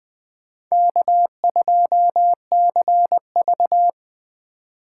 Morse code, heard as a keyed tone, about 20 wpm, 700 Hz. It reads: K2CV
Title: Speed Racing - USA 1x2 Call Signs 17wpm